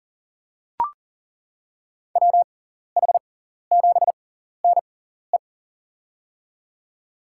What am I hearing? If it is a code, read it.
Morse code, heard as a keyed tone, about 40 wpm, spaced as for 14 wpm, 700 Hz. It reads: WH7NE
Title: USA Call Signs - 5 Characters - 6x Character Spacing - 40wpm